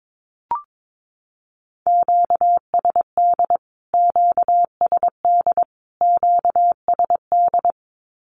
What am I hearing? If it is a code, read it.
Morse code, heard as a keyed tone, about 22 wpm, 700 Hz. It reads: QSD QSD QSD
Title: Q-Codes Repeated 3x 22wpm